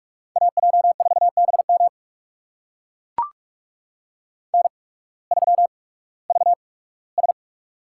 Morse code, heard as a keyed tone, about 44 wpm, spaced as for 21 wpm, 700 Hz. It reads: AJ4BM N3VS
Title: USA Call Signs - 8x Character Spacing - 45wpm